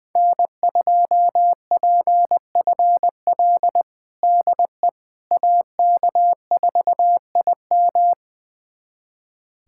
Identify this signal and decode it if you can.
Morse code, heard as a keyed tone, about 20 wpm, 700 Hz. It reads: N2PFL DE AK4IM
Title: USA Callsign DE Callsign - 5 Characters - 20wpm